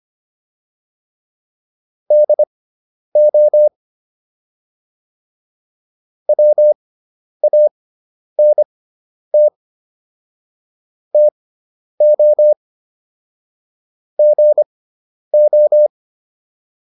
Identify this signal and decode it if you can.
Morse code, heard as a keyed tone, about 25 wpm, spaced as for 10 wpm, 600 Hz. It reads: DO WANT TO GO
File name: WD401 10